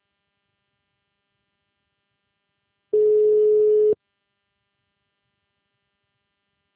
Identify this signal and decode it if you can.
Ringback tone (Europe, 425 Hz).